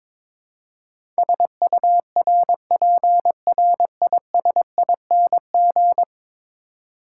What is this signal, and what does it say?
Morse code, heard as a keyed tone, about 22 wpm, 700 Hz. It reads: SURPRISING
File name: practice / 22wpm / 10 Letter Words - First 3 Letters Given - 22wpm